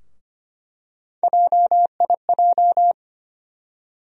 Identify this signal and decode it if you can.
Morse code, heard as a keyed tone, about 25 wpm, 700 Hz. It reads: JIJ
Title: USA Call Sign Postfixes - 3 Letters 25wpm